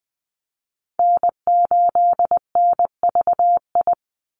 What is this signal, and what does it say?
Morse code, heard as a keyed tone, about 20 wpm, 700 Hz. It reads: N8NVI
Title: USA Call Signs 20wpm